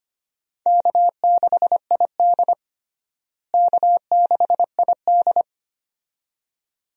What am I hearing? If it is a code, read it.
Morse code, heard as a keyed tone, about 25 wpm, 700 Hz. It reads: K6ID K6ID
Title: USA Call Signs - 4 Characters - Repeated 2x - 25wpm